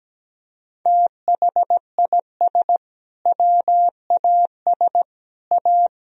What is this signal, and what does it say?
Morse code, heard as a keyed tone, about 17 wpm, 700 Hz. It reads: THIS WAS A
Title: Sentences from Top 1000 Words - Review Part 2 - 17wpm